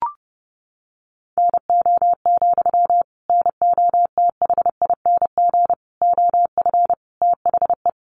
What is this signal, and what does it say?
Morse code, heard as a keyed tone, about 30 wpm, 700 Hz. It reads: NO, NOTHING OF THE
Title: Sentences from Top 500 Words 30wpm